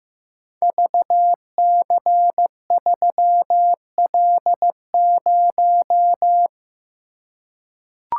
Morse code, heard as a keyed tone, about 15 wpm, 700 Hz. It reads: VC3L0
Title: Code Groups - 5 Letter:Numbers - 15wpm